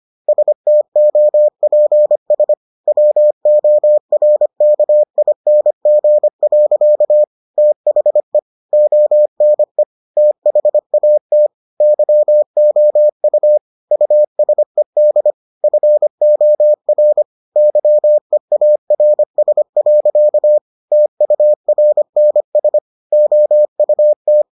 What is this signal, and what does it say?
Morse code, heard as a keyed tone, about 25 wpm, 600 Hz. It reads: STOPS WORKING. THE ONE THAT YOU USED FOR YEARS. TURNS OUT